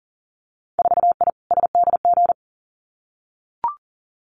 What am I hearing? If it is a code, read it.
Morse code, heard as a keyed tone, about 42 wpm, 700 Hz. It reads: 4I SDG
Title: Winter Field Day Exchanges Encore 40wpm